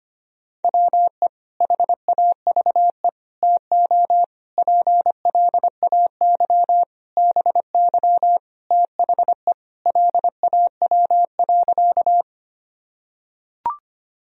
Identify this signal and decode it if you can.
Morse code, heard as a keyed tone, about 25 wpm, 700 Hz. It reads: WE HAVE TO PLAY BY THE LAW.
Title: Sentences from Top 1000 Words - Review Part 3 - 25wpm